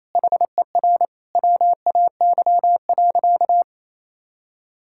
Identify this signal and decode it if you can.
Morse code, heard as a keyed tone, about 28 wpm, 700 Hz. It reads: HER WAY.
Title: Sentences from Top 1600 Words 28wpm